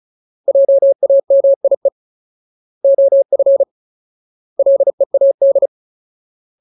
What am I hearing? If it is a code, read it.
Morse code, heard as a keyed tone, about 34 wpm, 550 Hz. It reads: JAMIE OF LEAD
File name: Ultimate Word List / 35WPM Ultimate